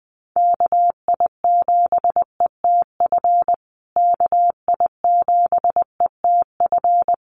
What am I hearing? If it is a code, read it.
Morse code, heard as a keyed tone, about 20 wpm, 700 Hz. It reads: KI7ETF KI7ETF
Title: USA Call Signs Repeated 3x 20wpm